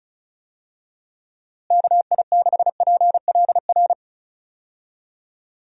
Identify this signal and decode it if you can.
Morse code, heard as a keyed tone, about 36 wpm, 700 Hz. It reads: KI6PLR
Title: USA Call Signs - 6 Characters - Repeated 2x - 35wpm